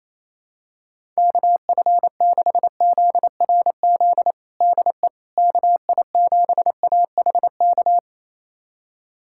Morse code, heard as a keyed tone, about 28 wpm, 700 Hz. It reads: KF6ZRZ DE KI7AHK
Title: USA Callsign DE Callsign 28wpm